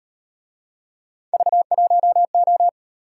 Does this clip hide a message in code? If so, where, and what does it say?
Yes: Morse code from 1.3 to 2.7 s, heard as a keyed tone, about 38 wpm, 700 Hz. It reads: V1O